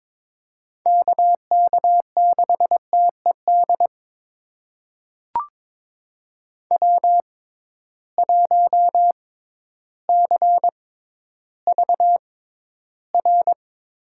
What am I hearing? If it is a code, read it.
Morse code, heard as a keyed tone, about 22 wpm, spaced as for 12 wpm, 700 Hz. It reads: KK6TED W1CVR
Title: USA Call Signs - 6x Character Spacing - 22wpm